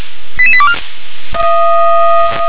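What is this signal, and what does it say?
Five-tone selective call (ZVEI), heard as quick runs of five tones, tones of about 70 ms. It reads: 88824
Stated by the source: ZVEI five-tone selective call received by SDR.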